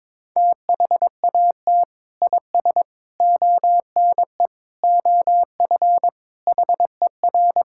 Morse code, heard as a keyed tone, about 22 wpm, 700 Hz. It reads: THAT IS ONE OF HER
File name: practice / 22wpm / Sentences with 2-Word Phrases - 22wpm